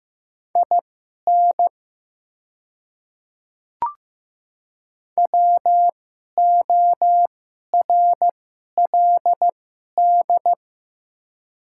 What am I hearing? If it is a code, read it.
Morse code, heard as a keyed tone, about 15 wpm, spaced as for 11 wpm, 700 Hz. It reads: IN WORLD
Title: 2x Character Spacing - Top 300 Words - 15wpm